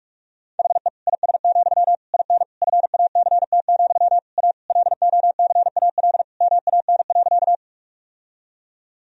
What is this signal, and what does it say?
Morse code, heard as a keyed tone, about 43 wpm, 700 Hz. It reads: HE IS, IN FACT, A LOCAL MAN.
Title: Sentences from Top 1000 Words - Review Part 3 - 45wpm